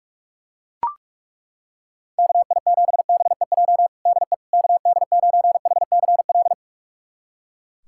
Morse code, heard as a keyed tone, about 41 wpm, 700 Hz. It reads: KI7BEJ DE KD9HCL
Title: USA Callsign DE Callsign - 6 Characters - 45wpm